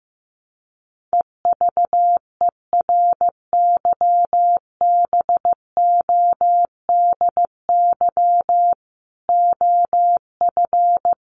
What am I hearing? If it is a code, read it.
Morse code, heard as a keyed tone, about 15 wpm, 700 Hz. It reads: EVERYBODY OF
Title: Sentences from Top 1400 Words 15wpm